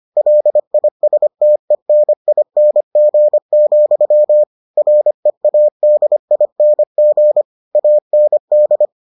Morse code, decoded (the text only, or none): LISTENING, READING AND